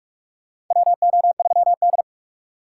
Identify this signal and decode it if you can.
Morse code, heard as a keyed tone, about 46 wpm, 700 Hz. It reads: WO3D
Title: USA Call Signs - 3x Character Spacing - 45wpm